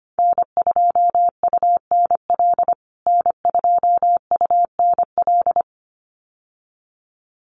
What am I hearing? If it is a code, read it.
Morse code, heard as a keyed tone, about 25 wpm, 700 Hz. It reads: N2UNL N2UNL
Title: USA Call Signs Repeated 2x 25wpm